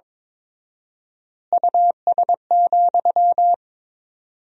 Morse code, heard as a keyed tone, about 21 wpm, 700 Hz. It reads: US,
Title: 4x Word-Spacing - Sentences from Top 400 Words - 22wpm